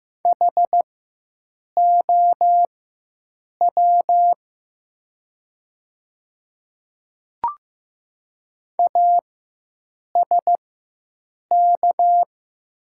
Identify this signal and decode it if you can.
Morse code, heard as a keyed tone, about 15 wpm, spaced as for 7 wpm, 700 Hz. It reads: HOW ASK
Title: 4x Character Spacing - Top 300 Words - 15wpm